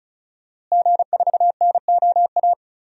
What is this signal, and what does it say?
Morse code, heard as a keyed tone, about 34 wpm, 700 Hz. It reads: G4NOA
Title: DXCC-100 Call Signs - 5 Characters - Repeated 3x - 35wpm